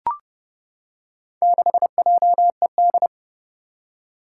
Morse code, heard as a keyed tone, about 32 wpm, 700 Hz. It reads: 6JED
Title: Four Challenging Letter-Number Combinations 30wpm